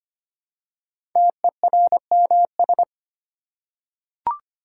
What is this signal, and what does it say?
Morse code, heard as a keyed tone, about 25 wpm, 700 Hz. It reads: TERMS
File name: 5 Letter Words - First 3 Letters Given - 25wpm